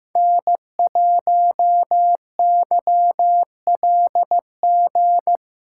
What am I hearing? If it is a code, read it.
Morse code, heard as a keyed tone, about 15 wpm, 700 Hz. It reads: N1YLG
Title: USA Callsign DE Callsign - 5 Characters - 15wpm